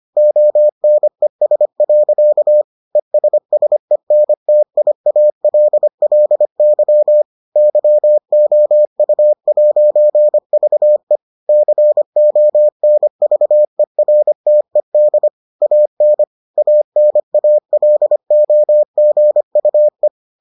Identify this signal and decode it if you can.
Morse code, heard as a keyed tone, about 25 wpm, 600 Hz. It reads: ONES. ESSENTIALLY YOU'VE CONVERTED AN ANALOGUE